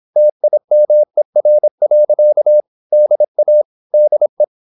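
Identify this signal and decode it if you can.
Morse code, heard as a keyed tone, about 26 wpm, 600 Hz. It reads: TIMER. DA DE
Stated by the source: High-speed (QRQ) Morse at 26 wpm.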